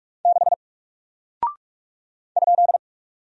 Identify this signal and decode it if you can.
Morse code, heard as a keyed tone, about 44 wpm, 700 Hz. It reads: 6 ?
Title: Single Character (Letter:Number:Punctuation:Prosign) 45wpm